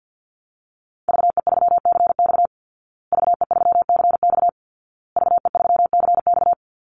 Morse code, heard as a keyed tone, about 50 wpm, 700 Hz. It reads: VE3CX VE3CX VE3CX
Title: Call Signs Repeated 2x - 50wpm